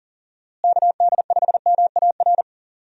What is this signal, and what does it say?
Morse code, heard as a keyed tone, about 40 wpm, 700 Hz. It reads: KD5MAR
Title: USA Call Signs - 6 Characters - Repeated 3x - 40wpm